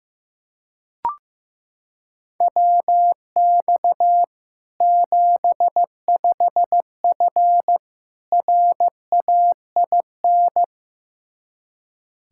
Morse code, heard as a keyed tone, about 15 wpm, 700 Hz. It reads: WX 75F RAIN
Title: Practice QSO - Exchange 2 - Info - 15wpm